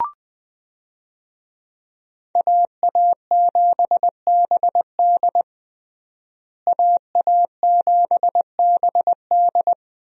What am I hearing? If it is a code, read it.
Morse code, heard as a keyed tone, about 20 wpm, 700 Hz. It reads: AA7BD AA7BD
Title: USA Call Signs - 5 Characters - Repeated 2x - 20wpm